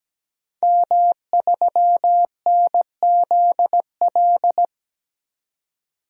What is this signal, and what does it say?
Morse code, heard as a keyed tone, about 17 wpm, 700 Hz. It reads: M3NZL
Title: DXCC-100 Call Signs - 5 Characters - 17wpm